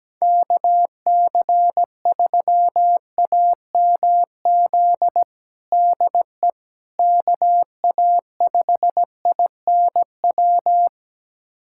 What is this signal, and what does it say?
Morse code, heard as a keyed tone, about 17 wpm, 700 Hz. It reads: KC3AMZ DE KA5INW
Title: USA Callsign DE Callsign - 6 Characters - 17wpm